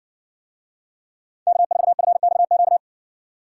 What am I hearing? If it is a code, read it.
Morse code, heard as a keyed tone, about 60 wpm, 700 Hz. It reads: K4FXQ